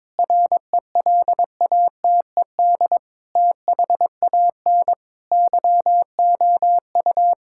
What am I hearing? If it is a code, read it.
Morse code, heard as a keyed tone, about 22 wpm, 700 Hz. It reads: RELATED THAN YOU
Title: Sentences from Top 1000 Words - Review Part 5 - 22wpm